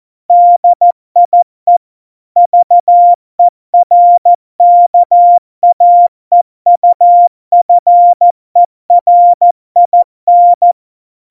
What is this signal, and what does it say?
Morse code, heard as a keyed tone, about 14 wpm, 700 Hz. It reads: DIE VERKAEUFERIN
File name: Morsekurs / Witze / Witz 03 April 2018, 70 F80